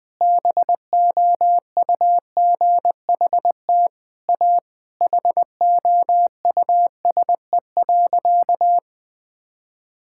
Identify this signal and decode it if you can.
Morse code, heard as a keyed tone, about 20 wpm, 700 Hz. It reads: BOUGHT A HOUSE.
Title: Sentences with 4-Word Phrases - 20wpm